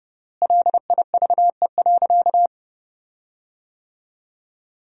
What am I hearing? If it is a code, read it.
Morse code, heard as a keyed tone, about 30 wpm, 700 Hz. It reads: LIVE.
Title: 3x Word-Spacing - Sentences from Top 300 Words - 30wpm